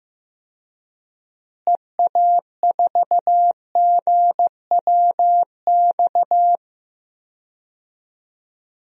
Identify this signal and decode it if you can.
Morse code, heard as a keyed tone, about 15 wpm, 700 Hz. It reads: EA4GWX
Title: DXCC-100 Call Signs - 6 Characters - Repeated 2x - 15wpm